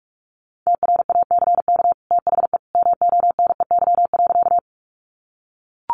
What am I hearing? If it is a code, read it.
Morse code, heard as a keyed tone, about 44 wpm, 700 Hz. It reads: TRACK THE MONEY.
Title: Sentences from Top 1100 Words 45wpm